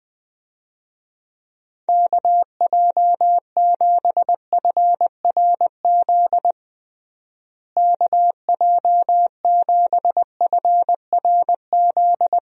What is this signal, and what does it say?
Morse code, heard as a keyed tone, about 20 wpm, 700 Hz. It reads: KJ7FRZ KJ7FRZ
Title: USA Call Signs - 6 Characters - Repeated 3x - 20wpm